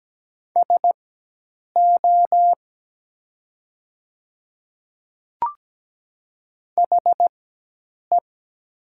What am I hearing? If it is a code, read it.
Morse code, heard as a keyed tone, about 17 wpm, spaced as for 8 wpm, 700 Hz. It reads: SO HE
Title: 4x Character Spacing - Top 100 Words - 17wpm